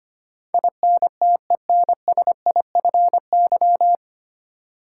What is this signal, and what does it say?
Morse code, heard as a keyed tone, about 25 wpm, 690 Hz. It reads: INTENSIFY
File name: practice / 25wpm / 9 Letter Words - First 3 Letters Given - 25wpm